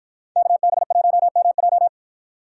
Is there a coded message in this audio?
Yes: Morse code from 0.4 to 1.9 s, heard as a keyed tone, about 63 wpm, 700 Hz. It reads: KB1MJ